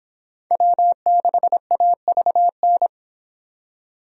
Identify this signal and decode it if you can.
Morse code, heard as a keyed tone, about 25 wpm, 700 Hz. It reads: W6AVN